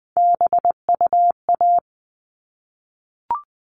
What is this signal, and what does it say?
Morse code, heard as a keyed tone, about 20 wpm, 700 Hz. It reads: BUA
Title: All Cut Numbers - 3-Digit 20wpm